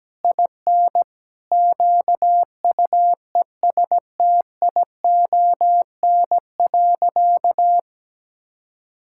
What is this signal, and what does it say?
Morse code, heard as a keyed tone, about 17 wpm, 700 Hz. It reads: IN QUESTION.
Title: Sentences from Top 1000 Words - Review Part 6 - 17wpm